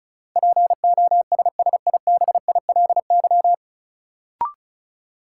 Morse code, heard as a keyed tone, about 34 wpm, 700 Hz. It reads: POSSIBILY
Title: Sentences from Top 1300 Words - New Word Drill - 35wpm